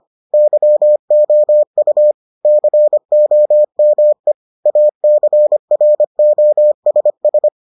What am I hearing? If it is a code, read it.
Morse code, heard as a keyed tone, about 25 wpm, 600 Hz. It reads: YOU COME ACROSS